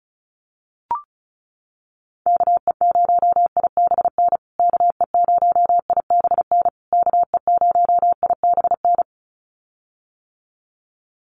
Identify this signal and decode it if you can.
Morse code, heard as a keyed tone, about 34 wpm, 700 Hz. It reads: KE0IBN KE0IBN KE0IBN
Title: USA Call Signs Repeated 3x 35wpm